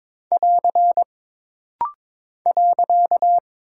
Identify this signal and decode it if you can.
Morse code, heard as a keyed tone, about 23 wpm, 700 Hz. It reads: <AR> .
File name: practice / 22wpm / Single Punctuation-Prosign - Rapid-Fire - 22wpm